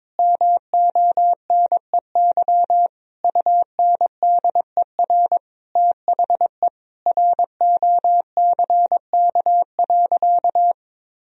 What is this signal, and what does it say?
Morse code, heard as a keyed tone, about 22 wpm, 700 Hz. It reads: MONEY UNDER THE ROCK.
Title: Sentences from Top 1000 Words - Review Part 2 - 22wpm